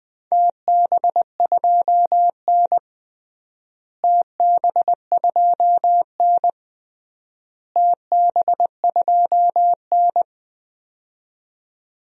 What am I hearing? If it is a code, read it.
Morse code, heard as a keyed tone, about 21 wpm, 700 Hz. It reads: TB2N TB2N TB2N
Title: DXCC-100 Call Signs - 4 Characters - Repeated 3x - 20wpm